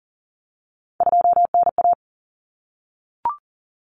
Morse code, heard as a keyed tone, about 40 wpm, 700 Hz. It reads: 2NA
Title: Common Cut Numbers - 3-Digit 40wpm